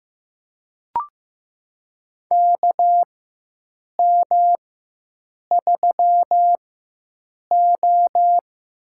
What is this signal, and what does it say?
Morse code, heard as a keyed tone, about 15 wpm, spaced as for 7 wpm, 700 Hz. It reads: KM3O
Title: USA Call Signs - 4x Character Spacing - 15wpm